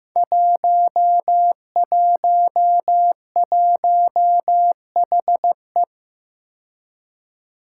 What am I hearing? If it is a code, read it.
Morse code, heard as a keyed tone, about 15 wpm, 700 Hz. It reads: 111HE
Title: Five Challenging Letter-Number Combinations 15wpm